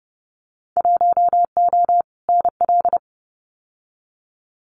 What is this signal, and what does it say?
Morse code, heard as a keyed tone, about 31 wpm, 700 Hz. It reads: 1O NL
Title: Winter Field Day Exchanges Encore 30wpm